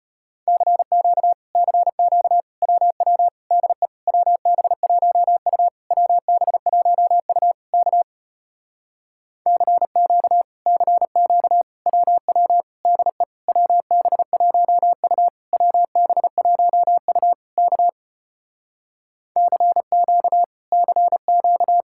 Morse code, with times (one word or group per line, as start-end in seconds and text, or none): 0.5-1.3 CQ
1.6-2.4 CQ
2.6-3.3 WW
3.5-3.9 DE
4.1-5.7 WB1U
5.9-7.5 WB1U
7.7-8.0 K
9.5-10.4 CQ
10.7-11.6 CQ
11.9-12.6 WW
12.9-13.2 DE
13.5-15.3 WB1U
15.5-17.3 WB1U
17.6-17.9 K
19.4-20.4 CQ
20.7-21.8 CQ